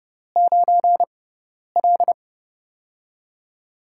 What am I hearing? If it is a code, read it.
Morse code, heard as a keyed tone, about 30 wpm, 700 Hz. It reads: 9 L
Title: USA Call Signs - 6x Character Spacing - 30wpm